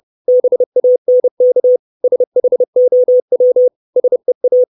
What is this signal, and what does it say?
Morse code, heard as a keyed tone, about 30 wpm, 500 Hz. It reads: BANK SHOW SEA
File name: 632 words / 632 Words 30WPM